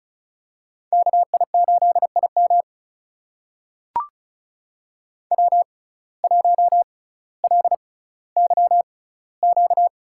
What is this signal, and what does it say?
Morse code, heard as a keyed tone, about 35 wpm, spaced as for 18 wpm, 700 Hz. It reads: KI8IM W1LYQ
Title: USA Call Signs - 5 Characters - 6x Character Spacing - 35wpm